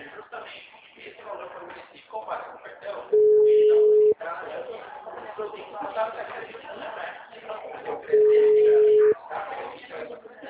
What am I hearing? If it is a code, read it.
Ringback tone (Europe, 425 Hz), with speech.
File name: out- -2003- - - .40352